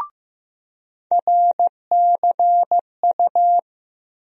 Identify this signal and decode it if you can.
Morse code, heard as a keyed tone, about 15 wpm, 700 Hz. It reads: RCU
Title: USA Call Sign Postfixes - 3 Letters 15wpm